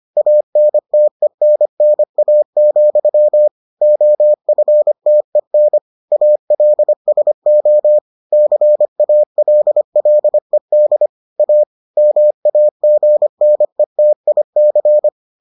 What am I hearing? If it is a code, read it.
Morse code, heard as a keyed tone, about 25 wpm, 600 Hz. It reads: ANTENNA, OFTEN ALSO CALLED A MAGNETIC